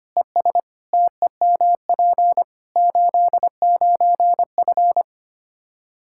Morse code, heard as a keyed tone, about 25 wpm, 700 Hz. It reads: ES TEMP 89F
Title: Practice QSO - Exchange 2 - Info - 25wpm